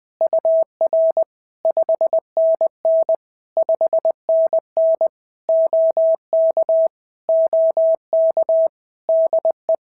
Morse code, heard as a keyed tone, about 20 wpm, 650 Hz. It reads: UR 5NN 5NN OK OK DE
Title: POTA QSO Practice - 20wpm